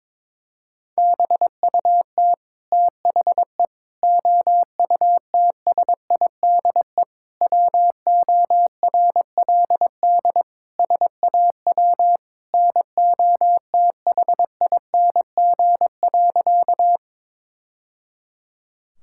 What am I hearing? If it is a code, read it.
Morse code, heard as a keyed tone, about 21 wpm, 700 Hz. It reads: BUT THE OUTSIDE WORLD SAW NOTHING.
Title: Sentences from Top 1000 Words - Review Part 6 - 22wpm